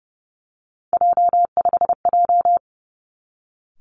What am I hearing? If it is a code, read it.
Morse code, heard as a keyed tone, about 32 wpm, 700 Hz. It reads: J5J
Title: Three Challenging Letter-Number Combinations 30wpm